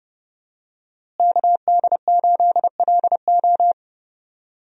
Morse code, heard as a keyed tone, about 31 wpm, 700 Hz. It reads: KD8LO
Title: USA Call Signs - 5 Characters - Repeated 3x - 30wpm